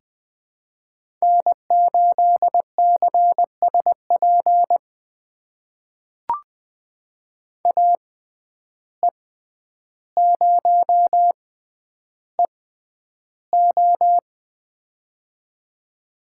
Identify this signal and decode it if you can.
Morse code, heard as a keyed tone, about 20 wpm, spaced as for 10 wpm, 700 Hz. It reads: N8CSP AE0EO
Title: USA Call Signs - 5 Characters - 6x Character Spacing - 20wpm